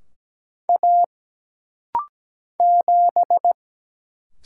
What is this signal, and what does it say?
Morse code, heard as a keyed tone, about 17 wpm, 700 Hz. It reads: A 7
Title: Single Letter-Number Alt - Rapid-Fire 17wpm